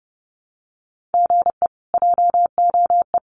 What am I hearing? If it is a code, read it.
Morse code, heard as a keyed tone, about 30 wpm, 700 Hz. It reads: GE JOE
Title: Common QSO Phrases Encore 30wpm